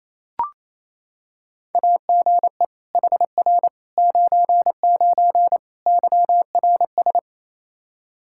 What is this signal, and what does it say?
Morse code, heard as a keyed tone, about 28 wpm, 700 Hz. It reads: AGE HR 99 YRS
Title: Practice QSO - Exchange 3 - More Info - 28wpm